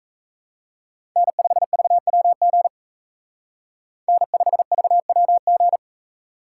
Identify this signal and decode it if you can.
Morse code, heard as a keyed tone, about 32 wpm, 700 Hz. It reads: N5VWG N5VWG
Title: Speed Racing - USA Call Signs 28wpm